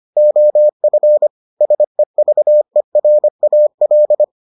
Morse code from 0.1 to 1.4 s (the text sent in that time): OF